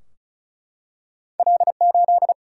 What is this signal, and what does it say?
Morse code, heard as a keyed tone, about 36 wpm, 700 Hz. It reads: L8
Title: Single-Letter Single-Number 35wpm